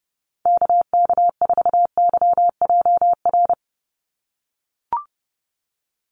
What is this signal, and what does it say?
Morse code, heard as a keyed tone, about 30 wpm, 700 Hz. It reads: KK4YJR
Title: USA Call Signs 30wpm